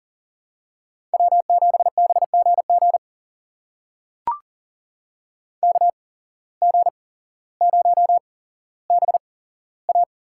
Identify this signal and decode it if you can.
Morse code, heard as a keyed tone, about 40 wpm, spaced as for 17 wpm, 700 Hz. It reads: W7BGG KG0BA
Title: USA Call Signs - 5 Characters - 8x Character Spacing - 40wpm